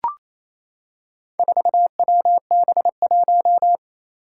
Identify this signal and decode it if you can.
Morse code, heard as a keyed tone, about 28 wpm, 700 Hz. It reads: 4WB1
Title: Four Challenging Letter-Number Combinations 28wpm